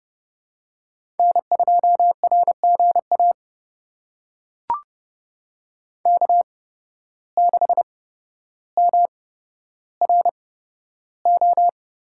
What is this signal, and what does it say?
Morse code, heard as a keyed tone, about 32 wpm, spaced as for 13 wpm, 700 Hz. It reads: N2RGA K6MRO
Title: USA Call Signs - 8x Character Spacing - 30wpm